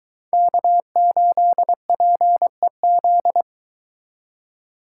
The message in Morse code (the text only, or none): K8PEZ